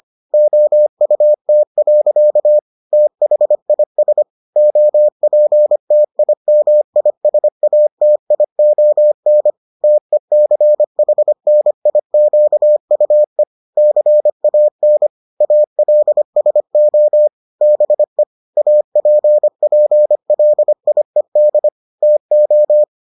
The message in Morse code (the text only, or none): OUT. THIS OPTIMISATION TECHNIQUE CAN ALSO BE APPLIED TO